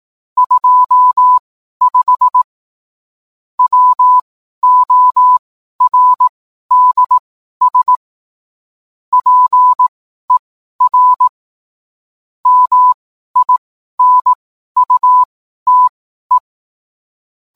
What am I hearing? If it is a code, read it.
Morse code, heard as a keyed tone, about 18 wpm, spaced as for 12 wpm, 1000 Hz. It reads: 25 WORDS PER MINUTE
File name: cwblog / cw mp3 directory / mem morse 12